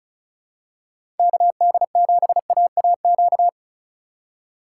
Morse code, heard as a keyed tone, about 34 wpm, 700 Hz. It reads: KD7AAQ
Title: USA Call Signs - 6 Characters - 4x Character Spacing - 35wpm